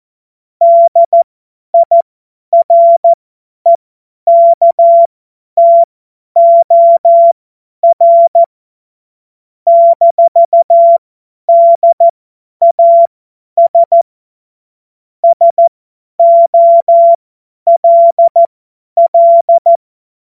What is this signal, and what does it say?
Morse code, heard as a keyed tone, about 14 wpm, spaced as for 10 wpm, 680 Hz. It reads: DIREKTOR - DAS SOLL
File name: Morsekurs / Witze / Witz 27. Dezember 2022 50 F70